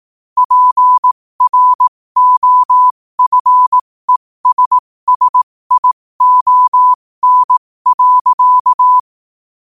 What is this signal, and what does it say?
Morse code, heard as a keyed tone, about 18 wpm, spaced as for 15 wpm, 1000 Hz. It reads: PROFESSION.
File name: qotdfiles / 15wpm